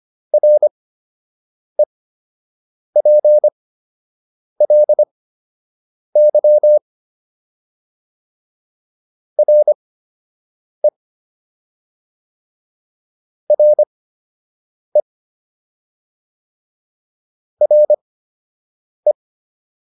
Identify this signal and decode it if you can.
Morse code, heard as a keyed tone, about 25 wpm, spaced as for 7 wpm, 600 Hz. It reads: REPLY RE RE RE